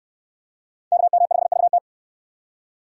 Morse code, heard as a keyed tone, about 66 wpm, 700 Hz. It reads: KM5VT